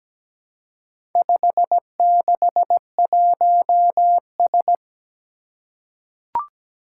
Morse code, heard as a keyed tone, about 17 wpm, 700 Hz. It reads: 561S
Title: Four Challenging Letter-Number Combinations 17wpm